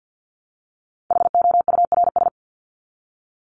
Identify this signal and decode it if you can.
Morse code, heard as a keyed tone, about 50 wpm, 700 Hz. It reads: HOURS
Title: ES:S Suffix - Top 100 Words with Base Words 50wpm